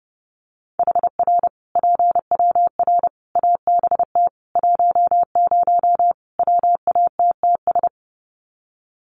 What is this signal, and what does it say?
Morse code, heard as a keyed tone, about 28 wpm, 700 Hz. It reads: HR PWR ABT 10 WATTS
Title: Common QSO Phrases 30wpm